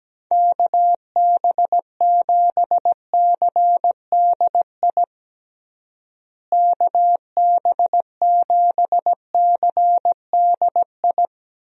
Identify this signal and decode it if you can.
Morse code, heard as a keyed tone, about 17 wpm, 700 Hz. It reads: KB7CDI KB7CDI
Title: USA Call Signs - 6 Characters - Repeated 3x - 17wpm